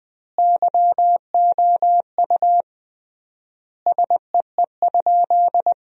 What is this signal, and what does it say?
Morse code, heard as a keyed tone, about 21 wpm, 700 Hz. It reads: YOU SEE?
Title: 3x Word-Spacing - Sentences from Top 100 Words - 20wpm